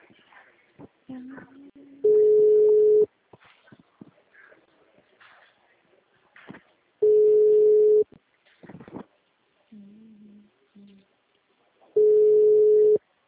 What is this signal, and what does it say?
Ringback tone (Europe, 425 Hz).